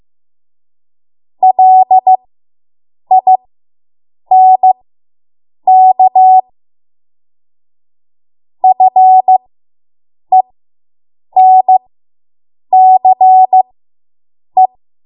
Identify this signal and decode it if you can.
Morse code, heard as a keyed tone, about 15 wpm, spaced as for 7 wpm, 750 Hz. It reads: LINK FENCE